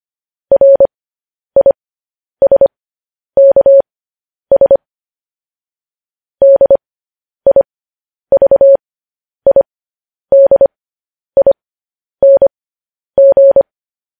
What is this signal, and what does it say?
Morse code, heard as a keyed tone, about 25 wpm, spaced as for 10 wpm, 550 Hz. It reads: RISKS DIVIDING
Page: news-headlines-in-morse-code-at-10-wpm